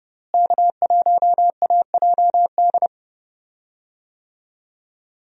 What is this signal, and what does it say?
Morse code, heard as a keyed tone, about 30 wpm, 700 Hz. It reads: K1AJD
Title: USA Call Signs - 5 Characters - Repeated 2x - 30wpm